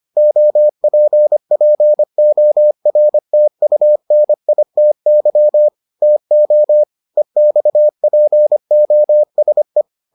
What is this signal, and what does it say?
Morse code, heard as a keyed tone, about 25 wpm, 600 Hz. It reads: OPPORTUNITY TO EXPOSE